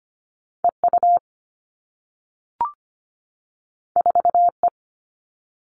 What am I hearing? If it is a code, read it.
Morse code, heard as a keyed tone, about 25 wpm, 700 Hz. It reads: EU 4E
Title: All Cut Numbers - 2-Digit 25wpm